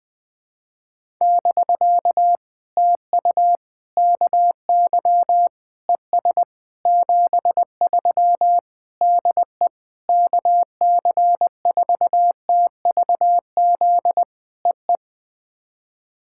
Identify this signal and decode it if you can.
Morse code, heard as a keyed tone, about 19 wpm, 700 Hz. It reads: <BK> TU KY ES 73 DE KC4TVZ EE
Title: POTA QSO Practice - 20wpm